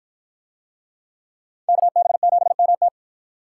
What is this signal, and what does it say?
Morse code, heard as a keyed tone, about 53 wpm, 700 Hz. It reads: KB7MT